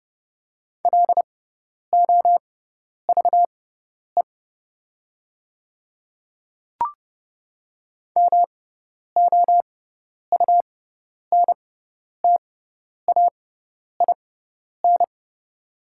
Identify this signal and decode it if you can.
Morse code, heard as a keyed tone, about 30 wpm, spaced as for 10 wpm, 690 Hz. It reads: LOVE MOUNTAIN
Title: 6x Character Spacing - Top 500 Words - 30wpm